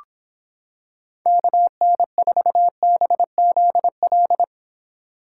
Morse code, heard as a keyed tone, about 26 wpm, 700 Hz. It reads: KN4BZL